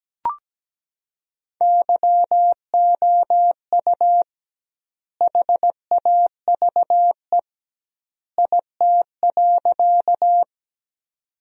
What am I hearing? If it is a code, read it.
Morse code, heard as a keyed tone, about 17 wpm, 700 Hz. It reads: YOU HAVE IT.
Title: 2x Word-Spacing - Sentences from Top 100 Words - 17wpm